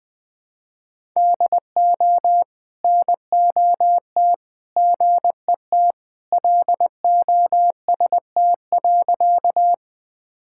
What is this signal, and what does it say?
Morse code, heard as a keyed tone, about 20 wpm, 700 Hz. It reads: DO NOT GET LOST.
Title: Sentences with 3-Word Phrases - 20wpm